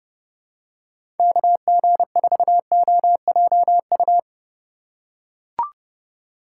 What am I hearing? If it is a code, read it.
Morse code, heard as a keyed tone, about 30 wpm, 700 Hz. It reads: KG4OJU
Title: USA Call Signs - 4x Character Spacing - 30wpm